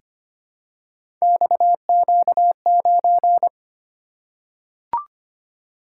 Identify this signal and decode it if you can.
Morse code, heard as a keyed tone, about 25 wpm, 700 Hz. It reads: XQ9
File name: Three Letter-Number Combinations - 25wpm